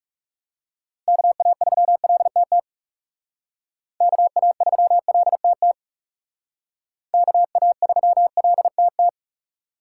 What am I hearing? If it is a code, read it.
Morse code, heard as a keyed tone, about 36 wpm, 700 Hz. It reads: KA3LTT KA3LTT KA3LTT
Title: Speed Racing - USA 2x3 Call Signs 30wpm